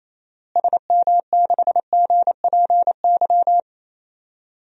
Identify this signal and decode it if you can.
Morse code, heard as a keyed tone, about 27 wpm, 700 Hz. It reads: SM6GPY
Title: DXCC-100 Call Signs - 6 Characters - Repeated 3x - 28wpm